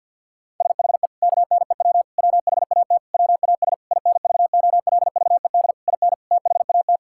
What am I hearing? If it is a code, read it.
Morse code, heard as a keyed tone, about 51 wpm, 700 Hz. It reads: SHE KNEW WHAT WAS INVOLVED IN THAT
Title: Sentences from Top 1000 Words - Review Part 5 - 50wpm